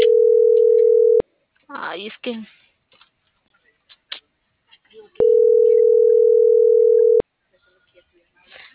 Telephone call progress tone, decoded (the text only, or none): ringback tone